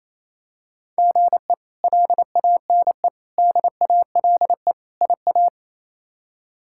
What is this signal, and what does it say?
Morse code, heard as a keyed tone, about 28 wpm, 700 Hz. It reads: GE LANE DALE IA
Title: K1USN SST QSO Practice - 28wpm